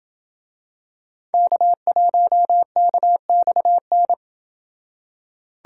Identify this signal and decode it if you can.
Morse code, heard as a keyed tone, about 27 wpm, 700 Hz. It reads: K1KXN